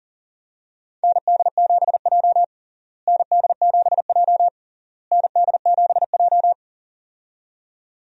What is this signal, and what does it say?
Morse code, heard as a keyed tone, about 40 wpm, 700 Hz. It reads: ND7J ND7J ND7J
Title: USA Call Signs - 4 Characters - Repeated 3x - 40wpm